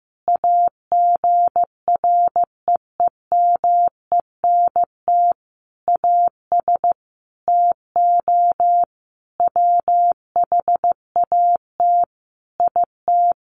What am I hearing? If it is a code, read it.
Morse code, heard as a keyed tone, about 14 wpm, 700 Hz. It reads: AGREEMENT AS TO WHAT IT
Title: Sentences from Top 1500 Words 15wpm